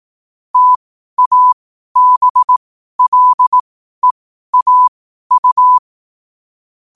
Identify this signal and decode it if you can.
Morse code, heard as a keyed tone, about 18 wpm, spaced as for 12 wpm, 1000 Hz. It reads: TABLEAU
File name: qotdfiles / 12wpm